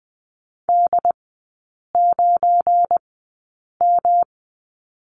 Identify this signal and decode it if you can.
Morse code, heard as a keyed tone, about 20 wpm, spaced as for 8 wpm, 700 Hz. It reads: D9M
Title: 184 Lesson 30e - Letter b - Three Characters 20wpm